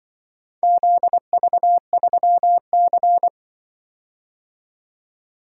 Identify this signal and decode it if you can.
Morse code, heard as a keyed tone, about 24 wpm, 700 Hz. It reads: ZV3C
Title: Speed-Racing - Four Letter-Number Combinations - 20wpm